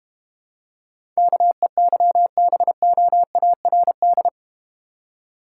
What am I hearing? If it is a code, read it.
Morse code, heard as a keyed tone, about 31 wpm, 700 Hz. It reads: KEYBOARD
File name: Easy Words - Speed-Racing with Hints - 25wpm